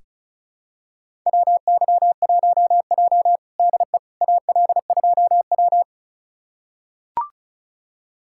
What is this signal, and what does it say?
Morse code, heard as a keyed tone, about 36 wpm, 700 Hz. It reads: WY1J DE AL2W
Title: USA Callsign DE Callsign - 4 Characters - 35wpm